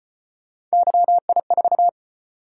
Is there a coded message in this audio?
Yes: Morse code from 0.7 to 1.9 s, heard as a keyed tone, about 34 wpm, 700 Hz. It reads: YI4